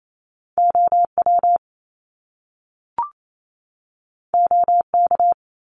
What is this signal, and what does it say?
Morse code, heard as a keyed tone, about 28 wpm, 700 Hz. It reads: OW OK
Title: Top Two Letter Combinations in Words 28wpm